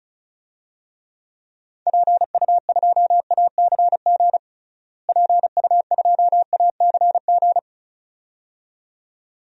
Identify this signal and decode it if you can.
Morse code, heard as a keyed tone, about 36 wpm, 700 Hz. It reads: PU2ACG PU2ACG
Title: DXCC-100 Call Signs - 6 Characters - Repeated 2x - 35wpm